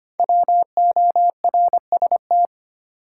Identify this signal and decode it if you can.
Morse code, heard as a keyed tone, about 25 wpm, 700 Hz. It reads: WORST
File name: 5 Letter Words - First 3 Letters Given - 25wpm